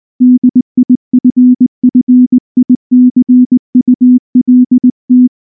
Morse code, heard as a keyed tone, about 19 wpm, 250 Hz. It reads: DIFFICULT
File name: morse / morse-training-20WPM-6